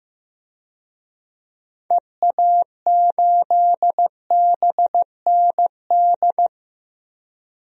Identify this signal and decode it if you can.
Morse code, heard as a keyed tone, about 16 wpm, 700 Hz. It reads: EA8BND
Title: DXCC-100 Call Signs - 6 Characters - Repeated 3x - 15wpm